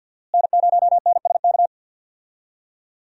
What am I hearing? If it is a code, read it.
Morse code, heard as a keyed tone, about 46 wpm, 700 Hz. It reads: N0NSK
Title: USA Call Signs - 5 Characters - Repeated 2x - 50wpm